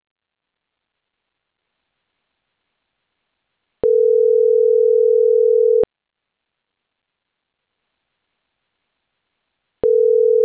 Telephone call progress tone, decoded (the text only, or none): ringback tone